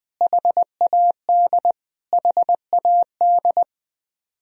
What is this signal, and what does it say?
Morse code, heard as a keyed tone, about 20 wpm, 700 Hz. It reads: HAD HAD
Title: Top 100 Words Repeated 20wpm